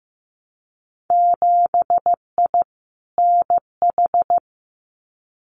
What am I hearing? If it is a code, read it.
Morse code, heard as a keyed tone, about 15 wpm, 700 Hz. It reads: 7I NH
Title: Winter Field Day Exchanges Encore 15wpm